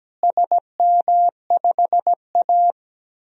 Morse code, heard as a keyed tone, about 17 wpm, 700 Hz. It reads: SM5A
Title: DXCC-100 Call Signs - 4 Characters - Repeated 3x - 17wpm